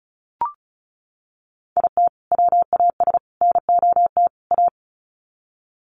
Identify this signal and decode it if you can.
Morse code, heard as a keyed tone, about 34 wpm, 700 Hz. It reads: IT WAS NOT A
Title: Sets of 4 Words 35wpm